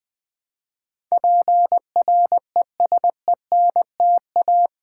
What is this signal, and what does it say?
Morse code, heard as a keyed tone, about 20 wpm, 700 Hz. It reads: PRESENTA
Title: Build A Word - 4 Letters At A Time - 12-Letter Words - 20wpm